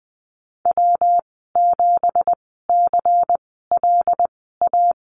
Morse code, heard as a keyed tone, about 20 wpm, spaced as for 15 wpm, 700 Hz. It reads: W7CLA
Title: 179 Lesson 29g - Slash - Call Signs 20wpm